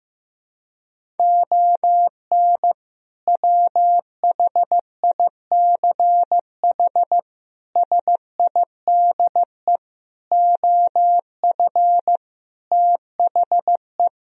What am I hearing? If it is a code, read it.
Morse code, heard as a keyed tone, about 15 wpm, 700 Hz. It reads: ON WHICH SIDE OF THE
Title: Sentences from Top 1000 Words - Review Part 1 - 15wpm